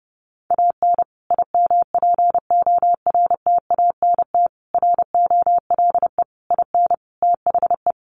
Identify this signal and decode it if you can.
Morse code, heard as a keyed tone, about 30 wpm, 700 Hz. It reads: AN IMPORTANT ROLE IN THE
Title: Sets of 5 Words Encore2 30wpm